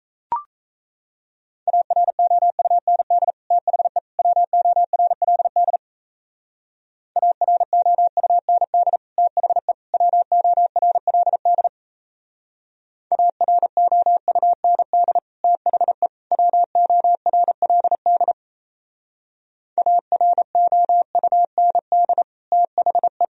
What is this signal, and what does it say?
Morse code, heard as a keyed tone, about 35 wpm, 700 Hz. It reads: AROUND THE WORLD AROUND THE WORLD AROUND THE WORLD AROUND THE
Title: Speed-Racing - Sets of 3 Words 28wpm